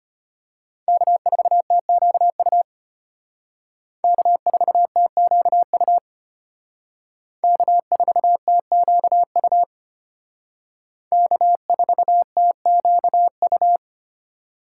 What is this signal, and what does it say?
Morse code, heard as a keyed tone, about 32 wpm, 700 Hz. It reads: K4TQU K4TQU K4TQU K4TQU